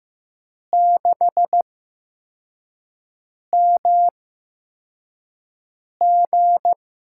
Morse code, heard as a keyed tone, about 15 wpm, spaced as for 4 wpm, 700 Hz. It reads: 6MG
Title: USA Call Signs - 5 Characters - 8x Character Spacing - 15wpm